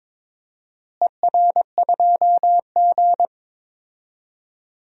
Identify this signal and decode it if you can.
Morse code, heard as a keyed tone, about 22 wpm, 700 Hz. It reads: ER2G
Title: DXCC-100 Call Signs - 4 Characters - Repeated 3x - 22wpm